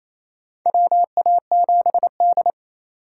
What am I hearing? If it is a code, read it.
Morse code, heard as a keyed tone, about 27 wpm, 690 Hz. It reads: WA7D
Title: USA Call Signs - 3x Character Spacing - 28wpm